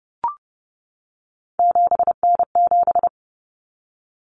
Morse code, heard as a keyed tone, about 30 wpm, 700 Hz. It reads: 7N7
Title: Common Cut Numbers - 3-Digit 30wpm